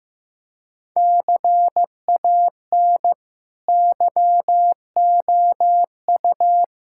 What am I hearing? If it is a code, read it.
Morse code, heard as a keyed tone, about 15 wpm, 700 Hz. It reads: CAN YOU
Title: Sentences from Top 1600 Words 15wpm